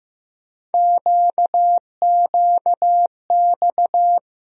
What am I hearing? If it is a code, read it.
Morse code, heard as a keyed tone, about 15 wpm, 700 Hz. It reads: QQX
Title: USA Call Sign Postfixes - 3 Letters 15wpm